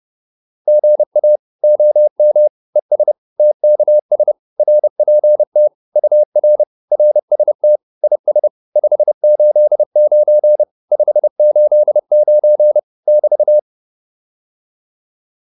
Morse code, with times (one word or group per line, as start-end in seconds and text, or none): 0.7-1.4 GA
1.6-2.5 OM
2.8-3.1 ES
3.4-4.3 TKS
4.6-5.7 RPT
6.0-6.6 UR
6.9-7.8 RST
8.0-8.5 IS
8.8-10.6 589
10.9-12.8 589
13.1-13.6 <BT>